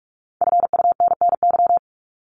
Morse code, heard as a keyed tone, about 44 wpm, 700 Hz. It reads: FUNNY
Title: Top 1751-2000 Words 45wpm